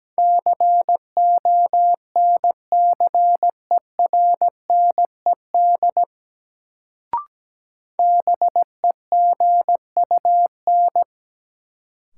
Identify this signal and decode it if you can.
Morse code, heard as a keyed tone, about 17 wpm, 700 Hz. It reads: CONCERNED BEGUN
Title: Sentences from Top 1000 Words - Review Part 6 - New Word Drill - 17wpm